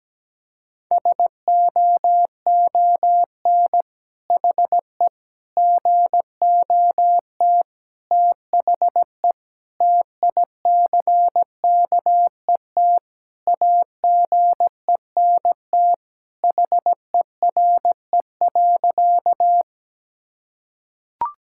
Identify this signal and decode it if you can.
Morse code, heard as a keyed tone, about 17 wpm, 700 Hz. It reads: SOON HE GOT THE TICKET AGENT HERE.
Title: Sentences from Top 1600 Words 17wpm